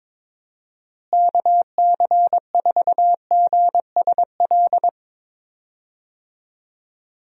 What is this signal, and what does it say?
Morse code, heard as a keyed tone, about 22 wpm, 700 Hz. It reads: KC4GSL
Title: USA Call Signs - 6 Characters - Repeated 2x - 22wpm